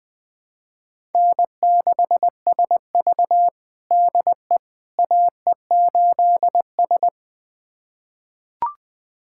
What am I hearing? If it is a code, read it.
Morse code, heard as a keyed tone, about 21 wpm, 700 Hz. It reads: N6SV DE AE8S
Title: USA Callsign DE Callsign - 4 Characters - 20wpm